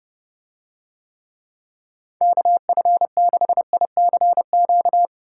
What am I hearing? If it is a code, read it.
Morse code, heard as a keyed tone, about 29 wpm, 700 Hz. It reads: KF6ICQ